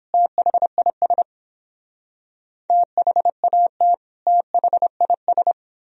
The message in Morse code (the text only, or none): THIS THAT THIS